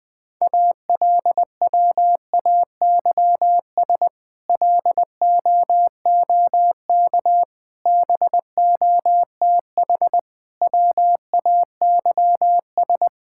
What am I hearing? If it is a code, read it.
Morse code, heard as a keyed tone, about 19 wpm, 700 Hz. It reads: ALWAYS LOOK BOTH WAYS
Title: Difficult Sentences with Hints - 20wpm